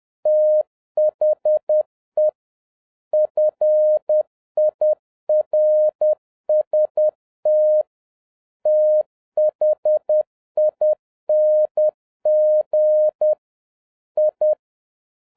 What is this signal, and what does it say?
Morse code, heard as a keyed tone, about 10 wpm, 610 Hz. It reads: THE FIRST THING I
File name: 2011-07-mp3 / n41 EXCITING QRP CW OPERATION 10wpm